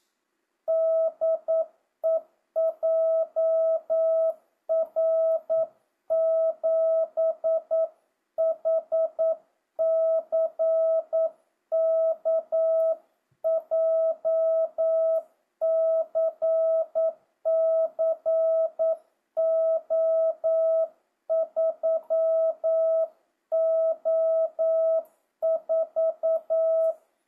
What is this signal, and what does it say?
Morse code, heard as a keyed tone, about 9 wpm, 650 Hz. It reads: DE JR7HCK JCC O3O4
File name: jh8cbh / 2025cw